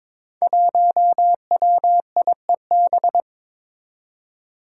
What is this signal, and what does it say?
Morse code, heard as a keyed tone, about 21 wpm, 700 Hz. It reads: 1WIEB
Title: Five Challenging Letter-Number Combinations 22wpm